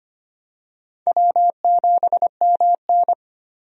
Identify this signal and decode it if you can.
Morse code, heard as a keyed tone, about 25 wpm, 700 Hz. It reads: W7MN
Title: USA Call Signs - 4 Characters - 25wpm